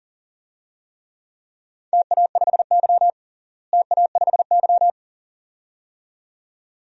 Morse code, heard as a keyed tone, about 40 wpm, 700 Hz. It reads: TA5Y TA5Y
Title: DXCC-100 Call Signs - 4 Characters - Repeated 2x - 40wpm